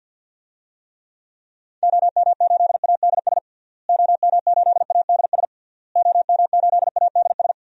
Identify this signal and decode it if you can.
Morse code, heard as a keyed tone, about 49 wpm, 700 Hz. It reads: OM8ADS OM8ADS OM8ADS
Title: DXCC-100 Call Signs - 6 Characters - Repeated 3x - 50wpm